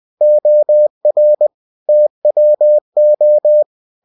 Morse code, heard as a keyed tone, about 20 wpm, 600 Hz. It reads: OR TWO
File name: cw-books / winnie-the-pooh / winnie-20wpm-page-0095